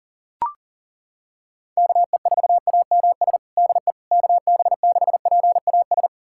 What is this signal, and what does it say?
Morse code, heard as a keyed tone, about 37 wpm, 700 Hz. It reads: KE4AMS DE KB6PAS
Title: USA Callsign DE Callsign 40wpm